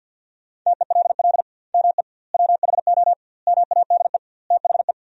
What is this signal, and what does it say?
Morse code, heard as a keyed tone, about 50 wpm, 700 Hz. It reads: TELL ME WHO MADE THE
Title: 2x Word-Spacing - Sentences from Top 200 Words - 50wpm